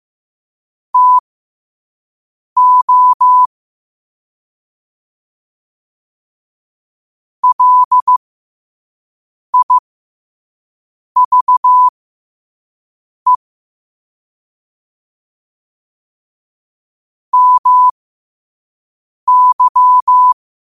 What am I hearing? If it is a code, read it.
Morse code, heard as a keyed tone, about 15 wpm, spaced as for 5 wpm, 1000 Hz. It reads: TO LIVE MY